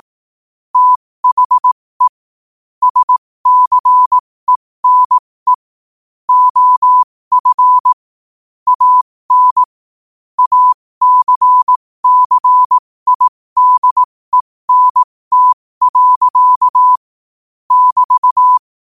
Morse code, heard as a keyed tone, about 18 wpm, spaced as for 15 wpm, 1000 Hz. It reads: THE SCENE OF AN ACCIDENT. <BT>
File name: qotdfiles / 15wpm